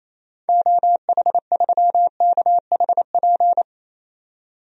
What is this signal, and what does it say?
Morse code, heard as a keyed tone, about 28 wpm, 700 Hz. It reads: OH3KHP
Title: DXCC-100 Call Signs - 6 Characters - 28wpm